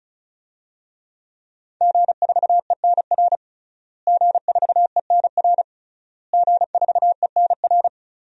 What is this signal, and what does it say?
Morse code, heard as a keyed tone, about 36 wpm, 700 Hz. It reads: G4ENR G4ENR G4ENR
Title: DXCC-100 Call Signs - 5 Characters - Repeated 3x - 35wpm